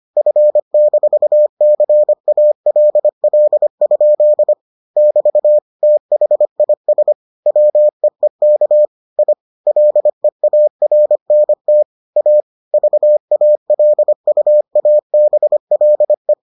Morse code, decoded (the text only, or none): F-CALL? <BT> THIS WEEK I LEARNT A VALUABLE